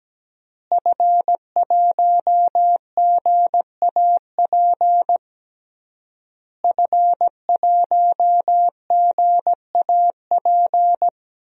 Morse code, heard as a keyed tone, about 17 wpm, 700 Hz. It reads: F1GAP F1GAP
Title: DXCC-100 Call Signs - 5 Characters - Repeated 3x - 17wpm